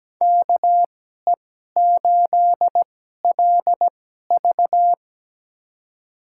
Morse code, heard as a keyed tone, about 17 wpm, spaced as for 12 wpm, 700 Hz. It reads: KE8LV
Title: USA Call Signs - 5 Characters - 2x Character Spacing - 17wpm